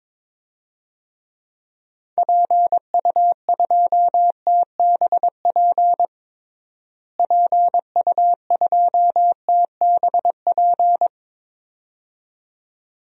Morse code, heard as a keyed tone, about 22 wpm, 700 Hz. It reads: PU2TBP PU2TBP
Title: DXCC-100 Call Signs - 6 Characters - Repeated 2x - 22wpm